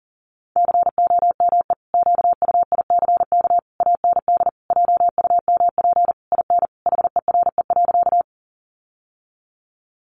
Morse code, heard as a keyed tone, about 37 wpm, 700 Hz. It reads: COME QUICK AND JUMP IN HERE.
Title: Sentences from Top 1300 Words 40wpm